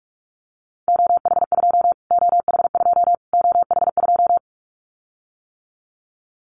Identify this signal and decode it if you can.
Morse code, heard as a keyed tone, about 44 wpm, 700 Hz. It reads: OH2 OH2 OH2
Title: Callsign Prefixes for Top 25 DXCC - Repeated 3x - 45wpm